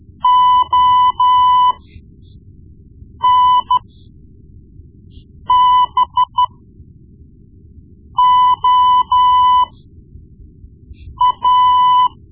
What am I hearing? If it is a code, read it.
Morse code, heard as a keyed tone, about 13 wpm, spaced as for 5 wpm, 1000 Hz. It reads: ONBOA
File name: Morse Lessons / Morse Lessons / Tape-1B